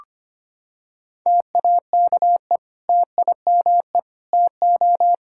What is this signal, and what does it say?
Morse code, heard as a keyed tone, about 25 wpm, 700 Hz. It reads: TAKE TIME TO
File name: practice / 25wpm / Sentences with 3-Word Phrases - 25wpm